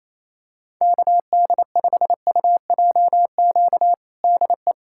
Morse code, heard as a keyed tone, about 28 wpm, 700 Hz. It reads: KD5UJQ DE
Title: USA Callsign DE Callsign - 6 Characters - 28wpm